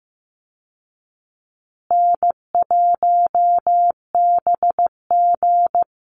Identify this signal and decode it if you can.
Morse code, heard as a keyed tone, about 14 wpm, 700 Hz. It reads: N1BG
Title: Easy Call Signs Repeated 2x - 15wpm